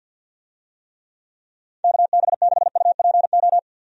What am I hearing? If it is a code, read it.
Morse code, heard as a keyed tone, about 51 wpm, 700 Hz. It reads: KB6UPO
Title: USA Call Signs - 6 Characters - 50wpm